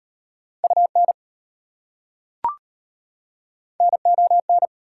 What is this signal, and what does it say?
Morse code, heard as a keyed tone, about 33 wpm, 700 Hz. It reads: UN NON